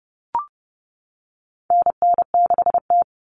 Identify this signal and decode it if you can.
Morse code, heard as a keyed tone, about 30 wpm, 700 Hz. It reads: NN6T
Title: Call Signs - 30wpm